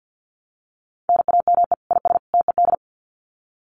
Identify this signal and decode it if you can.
Morse code, heard as a keyed tone, about 50 wpm, 700 Hz. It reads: NAME IS TED
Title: Common QSO Phrases 50wpm